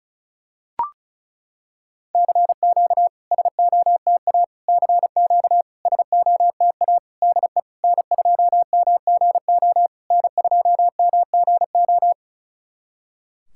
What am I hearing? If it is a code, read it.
Morse code, heard as a keyed tone, about 36 wpm, 700 Hz. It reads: CQ SOTA CQ SOTA DE N2MGO N2MGO
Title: SOTA QSO Practice 35wpm